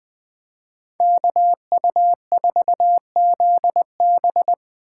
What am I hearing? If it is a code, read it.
Morse code, heard as a keyed tone, about 24 wpm, 700 Hz. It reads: KU4ZB
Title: Speed Racing - USA 2x2 Call Signs 20wpm